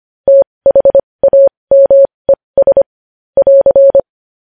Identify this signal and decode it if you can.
Morse code, heard as a keyed tone, about 25 wpm, spaced as for 20 wpm, 550 Hz. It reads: THAMES <AR>
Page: news-headlines-in-morse-code-at-20-wpm